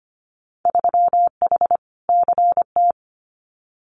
Morse code, heard as a keyed tone, about 25 wpm, 700 Hz. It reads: 3H CT
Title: Winter Field Day Exchanges Encore 25wpm